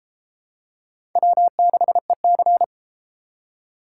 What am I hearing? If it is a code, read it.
Morse code, heard as a keyed tone, about 29 wpm, 700 Hz. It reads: W6EC